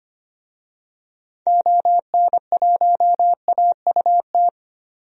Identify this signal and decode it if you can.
Morse code, heard as a keyed tone, about 25 wpm, 700 Hz. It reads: ON1AUT
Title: DXCC-100 Call Signs - 6 Characters - 25wpm